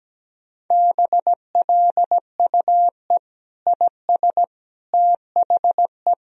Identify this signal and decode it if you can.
Morse code, heard as a keyed tone, about 17 wpm, 700 Hz. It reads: BLUE IS THE
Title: Sentences from Top 1000 Words - Review Part 3 - 17wpm